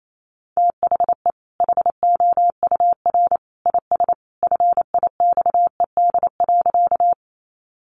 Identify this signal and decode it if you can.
Morse code, heard as a keyed tone, about 28 wpm, 700 Hz. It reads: THE HOUR IS FIXED.
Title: Sentences from Top 1100 Words 28wpm